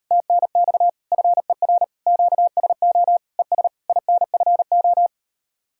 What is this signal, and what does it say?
Morse code, heard as a keyed tone, about 39 wpm, 700 Hz. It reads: TNX FER QSO ES INFO